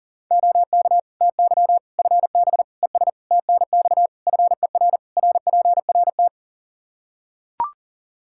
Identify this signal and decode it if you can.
Morse code, heard as a keyed tone, about 40 wpm, 700 Hz. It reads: OK TY FB ES TNX FER RPRT
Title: Practice QSO - Exchange 2 - Info - 40wpm